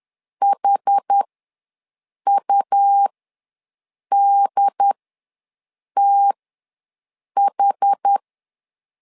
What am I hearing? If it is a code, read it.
Morse code, heard as a keyed tone, about 10 wpm, spaced as for 6 wpm, 800 Hz. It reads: HUDTH
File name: LEZIONI / 2 UHDN / 2 UHD ws 10wpm 2058